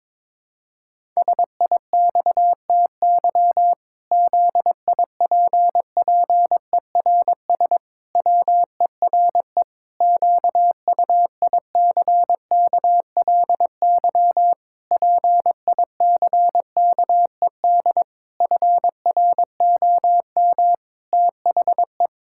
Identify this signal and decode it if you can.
Morse code, heard as a keyed tone, about 22 wpm, 700 Hz. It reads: SIXTY ZIPPERS WERE QUICKLY PICKED FROM THE
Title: Pangram Challenge v2 - 22wpm